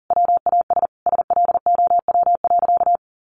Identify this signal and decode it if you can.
Morse code, heard as a keyed tone, about 40 wpm, 700 Hz. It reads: WAS SLOW.
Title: Sentences from Top 1400 Words 40wpm